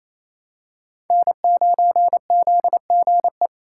Morse code, heard as a keyed tone, about 28 wpm, 700 Hz. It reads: N9ZGE
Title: USA Call Signs - 6x Character Spacing - 28wpm